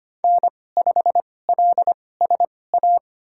Morse code, heard as a keyed tone, about 25 wpm, spaced as for 18 wpm, 700 Hz. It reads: N5LSA
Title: USA Call Signs - 5 Characters - 2x Character Spacing - 25wpm